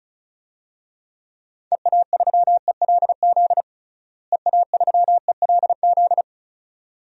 Morse code, heard as a keyed tone, about 34 wpm, 700 Hz. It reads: EA3ELZ EA3ELZ
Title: DXCC-100 Call Signs - 6 Characters - Repeated 2x - 35wpm